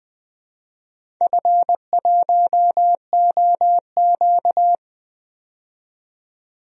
Morse code, heard as a keyed tone, about 20 wpm, 700 Hz. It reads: F1OQ
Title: DXCC-100 Call Signs - 4 Characters - 20wpm